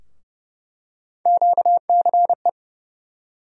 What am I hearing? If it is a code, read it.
Morse code, heard as a keyed tone, about 30 wpm, 700 Hz. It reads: QCE
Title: USA Call Sign Postfixes - 3 Letters 30wpm